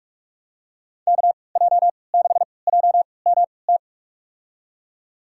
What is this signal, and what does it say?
Morse code, heard as a keyed tone, about 41 wpm, spaced as for 25 wpm, 700 Hz. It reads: KJ6JMT
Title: USA Call Signs - 6 Characters - 3x Character Spacing - 45wpm